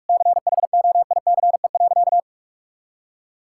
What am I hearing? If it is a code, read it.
Morse code, heard as a keyed tone, about 42 wpm, 700 Hz. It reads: CHOICE.
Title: Sentences from Top 1000 Words - Review Part 6 - 45wpm